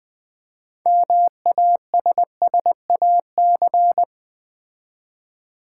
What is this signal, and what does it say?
Morse code, heard as a keyed tone, about 20 wpm, 700 Hz. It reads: MASSAC
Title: Build A Word - 2 Letters At A Time - 8-Letter Words - 20wpm